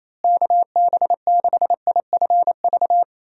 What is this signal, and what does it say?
Morse code, heard as a keyed tone, about 28 wpm, 700 Hz. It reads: KB6IFV
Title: USA Call Signs - 6 Characters - Repeated 3x - 28wpm